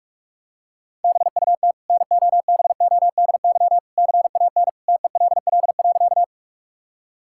Morse code, heard as a keyed tone, about 44 wpm, 700 Hz. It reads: BUT NOBODY CAN TELL.
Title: Sentences from Top 1600 Words 45wpm